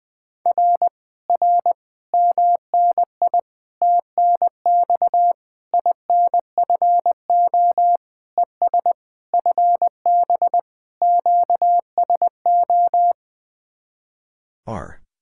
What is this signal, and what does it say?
Morse code, heard as a keyed tone, about 20 wpm, 700 Hz. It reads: R R MNI TNX INFO ES FB QSO
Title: Speed-Racing Common QSO Phrases - 20wpm